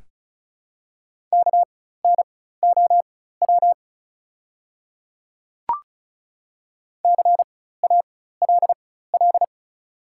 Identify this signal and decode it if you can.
Morse code, heard as a keyed tone, about 36 wpm, spaced as for 16 wpm, 700 Hz. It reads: KNOW CALL
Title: 4x Character Spacing - Top 100 Words - 35wpm